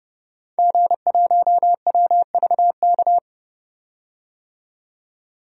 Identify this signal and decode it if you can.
Morse code, heard as a keyed tone, about 30 wpm, 700 Hz. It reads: G1WVK
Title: DXCC-100 Call Signs - 5 Characters - 30wpm